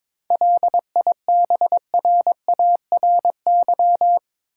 Morse code, heard as a keyed tone, about 24 wpm, 700 Hz. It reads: LIBRARY
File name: Easy Words - Speed-Racing with Hints - 22wpm